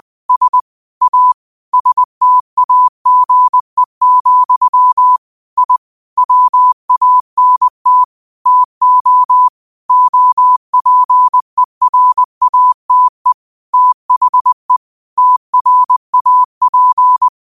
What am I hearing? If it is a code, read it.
Morse code, heard as a keyed tone, about 20 wpm, 1000 Hz. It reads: S A STAGE, I WANT TO OPERATE THE TRAP
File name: qotdfiles / 20wpm